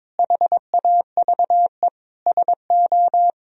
Morse code, heard as a keyed tone, about 22 wpm, 700 Hz. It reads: HAVE SO
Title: Sentences from Top 1000 Words - Review Part 1 - 22wpm